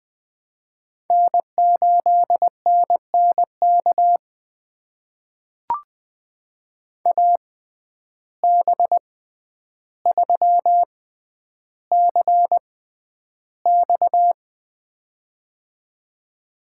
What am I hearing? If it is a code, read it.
Morse code, heard as a keyed tone, about 20 wpm, spaced as for 10 wpm, 700 Hz. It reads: N8NNK AB3CX
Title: USA Call Signs - 5 Characters - 6x Character Spacing - 20wpm